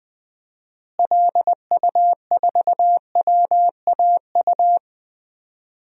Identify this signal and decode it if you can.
Morse code, heard as a keyed tone, about 20 wpm, 700 Hz. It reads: LU4WAU
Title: DXCC-100 Call Signs - 6 Characters - Repeated 3x - 20wpm